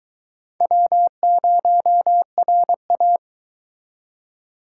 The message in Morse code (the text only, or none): W0RA